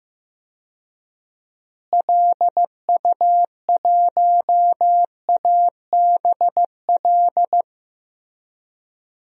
Morse code, heard as a keyed tone, about 15 wpm, 700 Hz. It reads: LU1ABL
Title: DXCC-100 Call Signs - 6 Characters - Repeated 3x - 15wpm